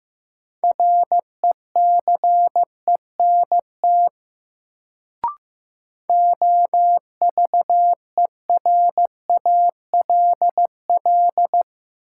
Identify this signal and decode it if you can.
Morse code, heard as a keyed tone, about 15 wpm, 700 Hz. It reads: RECENT OVERALL
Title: Sentences from Top 1000 Words - Review Part 7 - New Word Drill - 15wpm